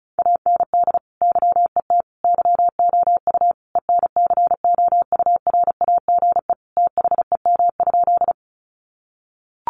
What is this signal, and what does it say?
Morse code, heard as a keyed tone, about 34 wpm, 700 Hz. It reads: AND YET YOU ENCOURAGE THEM?
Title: Sentences from Top 1400 Words 35wpm